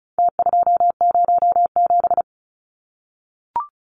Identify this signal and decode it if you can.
Morse code, heard as a keyed tone, about 34 wpm, 700 Hz. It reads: T207
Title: Numbers - Contest Style - 4-Digit 35wpm